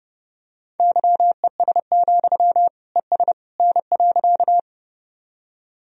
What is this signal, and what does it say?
Morse code, heard as a keyed tone, about 30 wpm, 700 Hz. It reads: YES, ES N.
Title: Sentences from Top 500 Words and Abbreviations 30wpm